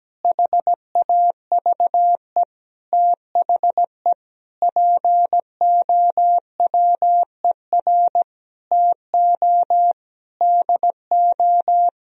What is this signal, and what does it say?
Morse code, heard as a keyed tone, about 17 wpm, 700 Hz. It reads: HAVE THE POWER TO DO
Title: Sentences from Top 1000 Words - Review Part 4 - 17wpm